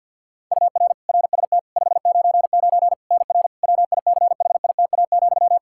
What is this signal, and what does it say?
Morse code, heard as a keyed tone, about 51 wpm, 700 Hz. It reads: UR RST 599 NR WICHITA,
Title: K1USN SST QSO Practice 50wpm